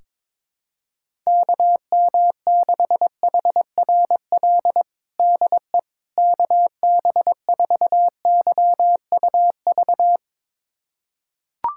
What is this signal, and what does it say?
Morse code, heard as a keyed tone, about 22 wpm, 700 Hz. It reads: KM6HRL DE KB4YUV
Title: USA Callsign DE Callsign 22wpm